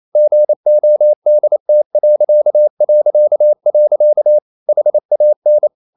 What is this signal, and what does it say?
Morse code, heard as a keyed tone, about 28 wpm, 600 Hz. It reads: GODT... HAN
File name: Ferie QRQ 28wpm